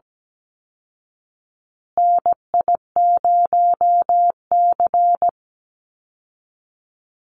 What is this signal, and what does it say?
Morse code, heard as a keyed tone, about 17 wpm, 700 Hz. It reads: NI0C
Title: Call Signs Repeated 3x - 17wpm